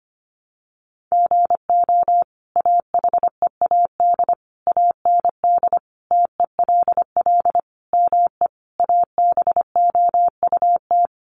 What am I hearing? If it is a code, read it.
Morse code, heard as a keyed tone, about 25 wpm, 700 Hz. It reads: GO AHEAD AND TELL ME ABOUT
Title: Sentences from Top 1100 Words 25wpm